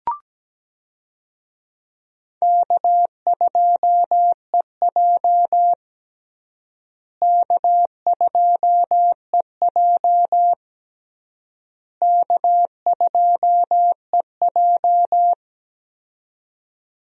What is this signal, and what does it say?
Morse code, heard as a keyed tone, about 17 wpm, 700 Hz. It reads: K2EJ K2EJ K2EJ
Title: USA Call Signs - 4 Characters - Repeated 3x - 17wpm